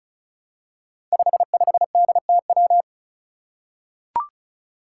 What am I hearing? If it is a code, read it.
Morse code, heard as a keyed tone, about 36 wpm, 700 Hz. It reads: 55DTW
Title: Five Challenging Letter-Number Combinations 35wpm